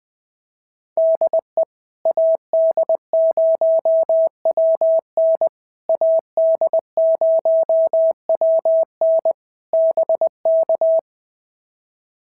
Morse code, heard as a keyed tone, about 20 wpm, 650 Hz. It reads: DE AD0WN AD0WN BK
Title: POTA QSO Practice - 20wpm